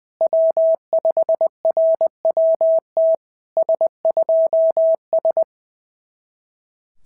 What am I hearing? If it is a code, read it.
Morse code, heard as a keyed tone, about 19 wpm, 650 Hz. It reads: W5RWT S2S
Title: SOTA QSO Practice 20wpm